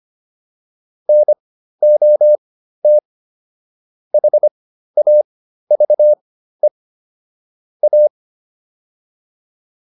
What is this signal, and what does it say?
Morse code, heard as a keyed tone, about 25 wpm, spaced as for 13 wpm, 600 Hz. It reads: NOT HAVE A